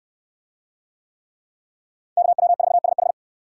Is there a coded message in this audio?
Yes: Morse code from 2.2 to 3.1 s, heard as a keyed tone, about 68 wpm, 700 Hz. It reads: KW4SH